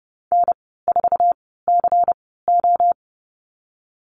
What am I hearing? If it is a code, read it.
Morse code, heard as a keyed tone, about 30 wpm, spaced as for 17 wpm, 700 Hz. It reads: N4CO
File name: Lesson 19g - Letter w - Call Signs 30wpm